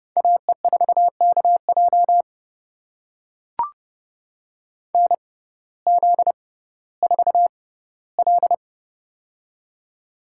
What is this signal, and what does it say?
Morse code, heard as a keyed tone, about 31 wpm, spaced as for 17 wpm, 700 Hz. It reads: AE4KJ NZ4L
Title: USA Call Signs - 6x Character Spacing - 30wpm